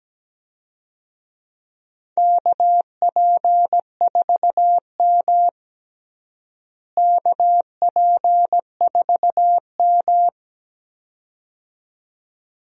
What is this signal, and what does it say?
Morse code, heard as a keyed tone, about 17 wpm, 700 Hz. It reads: KP4M KP4M
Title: DXCC-100 Call Signs - 4 Characters - Repeated 2x - 17wpm